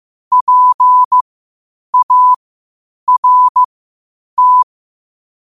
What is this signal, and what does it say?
Morse code, heard as a keyed tone, about 15 wpm, spaced as for 8 wpm, 1000 Hz. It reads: PART
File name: qotdfiles / 8wpm